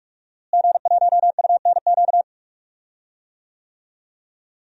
Morse code, heard as a keyed tone, about 47 wpm, 700 Hz. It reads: G1UNQ
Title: DXCC-100 Call Signs - 5 Characters - Repeated 3x - 45wpm